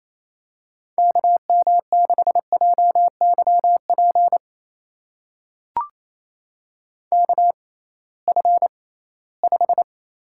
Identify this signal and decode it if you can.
Morse code, heard as a keyed tone, about 28 wpm, spaced as for 18 wpm, 700 Hz. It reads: KM6JYP KF5
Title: USA Call Signs - 6 Characters - 6x Character Spacing - 28wpm